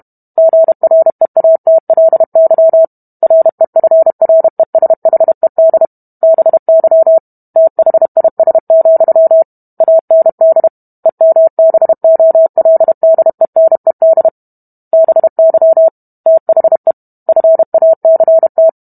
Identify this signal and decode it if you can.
Morse code, heard as a keyed tone, about 32 wpm, 650 Hz. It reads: GREATLY REFRESHED BY THIS, AND EMBOLDENED BY THE FACT